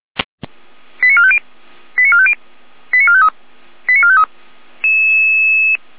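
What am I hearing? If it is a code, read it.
Five-tone selective call (ZVEI), heard as quick runs of five tones, tones of about 70 ms. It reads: 89450 89450 89453 89453
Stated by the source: ZVEI five-tone selective call 88076 received by SDR.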